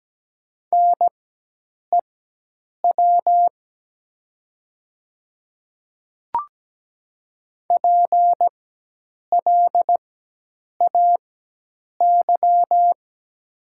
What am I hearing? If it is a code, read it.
Morse code, heard as a keyed tone, about 17 wpm, spaced as for 8 wpm, 700 Hz. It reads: NEW PLAY
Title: 4x Character Spacing - Top 200 Words - 17wpm